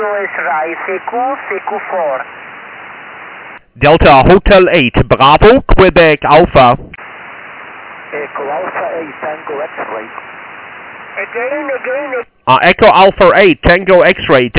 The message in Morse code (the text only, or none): none